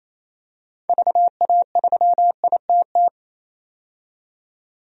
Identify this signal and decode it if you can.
Morse code, heard as a keyed tone, about 28 wpm, 700 Hz. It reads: VA3ITT
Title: DXCC-100 Call Signs - 6 Characters - 28wpm